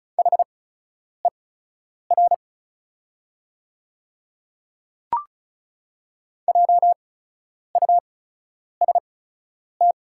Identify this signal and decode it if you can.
Morse code, heard as a keyed tone, about 36 wpm, spaced as for 10 wpm, 700 Hz. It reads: HER JUST
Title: 8x Character Spacing - Top 200 Words - 35wpm